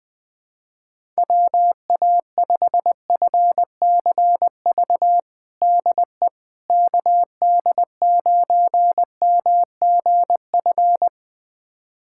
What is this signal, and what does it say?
Morse code, heard as a keyed tone, about 21 wpm, 700 Hz. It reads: WA5FCV DE KD9MGF
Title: USA Callsign DE Callsign - 6 Characters - 20wpm